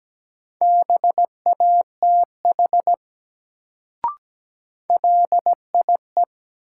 Morse code, heard as a keyed tone, about 17 wpm, 700 Hz. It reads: BATH LIE
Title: Sentences from Top 1600 Words - New Word Drill - 17wpm